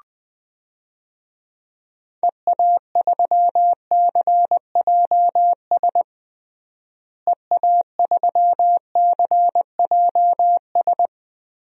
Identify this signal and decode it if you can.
Morse code, heard as a keyed tone, about 21 wpm, 700 Hz. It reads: EA3CJS EA3CJS
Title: DXCC-100 Call Signs - 6 Characters - Repeated 2x - 20wpm